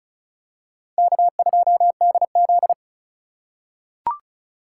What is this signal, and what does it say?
Morse code, heard as a keyed tone, about 36 wpm, 700 Hz. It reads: K2DZ
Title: USA Call Signs - 2x Character Spacing - 35wpm